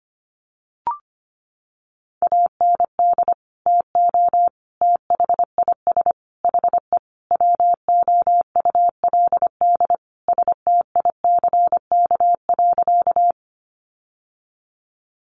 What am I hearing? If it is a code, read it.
Morse code, heard as a keyed tone, about 25 wpm, 700 Hz. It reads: AND TO THIS HE WOULD STICK.
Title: Sentences from Top 1200 Words 25wpm